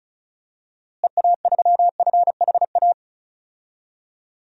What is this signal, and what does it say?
Morse code, heard as a keyed tone, about 36 wpm, 700 Hz. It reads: EA3FHA
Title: DXCC-100 Call Signs - 6 Characters - Repeated 3x - 35wpm